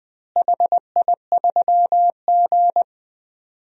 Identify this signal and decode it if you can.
Morse code, heard as a keyed tone, about 20 wpm, 700 Hz. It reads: HI3G
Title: DXCC-100 Call Signs - 4 Characters - Repeated 2x - 20wpm